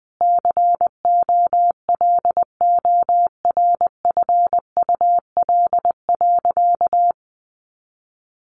Morse code, heard as a keyed tone, about 20 wpm, 700 Hz. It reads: COLORFUL.
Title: Easy Phrases 20wpm